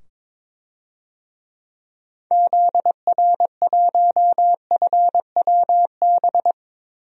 Morse code, heard as a keyed tone, about 22 wpm, 700 Hz. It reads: ZR1FWB
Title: DXCC-100 Call Signs - 6 Characters - 22wpm